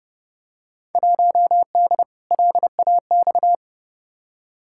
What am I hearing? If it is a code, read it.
Morse code, heard as a keyed tone, about 30 wpm, 700 Hz. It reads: 1D LAX
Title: Field Day Exchanges - 30wpm